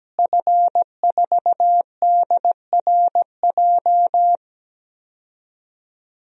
Morse code, heard as a keyed tone, about 17 wpm, 700 Hz. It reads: F4DRJ
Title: DXCC-100 Call Signs - 5 Characters - Repeated 3x - 17wpm